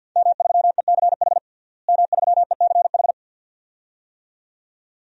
Morse code, heard as a keyed tone, about 53 wpm, 700 Hz. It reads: M3ECH M3ECH
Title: DXCC-100 Call Signs - 5 Characters - Repeated 2x - 50wpm